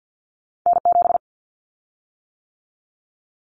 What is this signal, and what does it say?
Morse code, heard as a keyed tone, about 50 wpm, 700 Hz. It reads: N7
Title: Callsign Prefixes for Top 25 DXCC - 50wpm